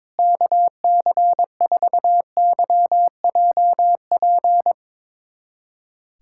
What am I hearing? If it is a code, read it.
Morse code, heard as a keyed tone, about 22 wpm, 700 Hz. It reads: KC4YJP
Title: USA Callsign DE Callsign 22wpm